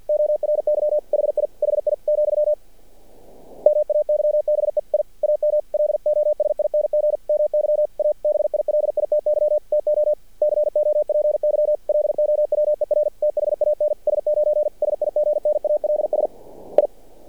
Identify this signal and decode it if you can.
Morse code, heard as a keyed tone, about 50 wpm, 600 Hz. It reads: ORY HI HI , MAYBE I AM LOSING MY ABILITY TO COPY LOWER THAN S9 SIGNALS I
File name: archived / CRSnet